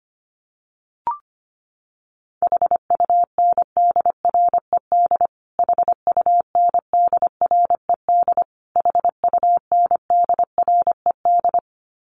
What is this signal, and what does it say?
Morse code, heard as a keyed tone, about 25 wpm, 700 Hz. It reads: HUNDRED HUNDRED HUNDRED
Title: Top 500 Words Repeated 25wpm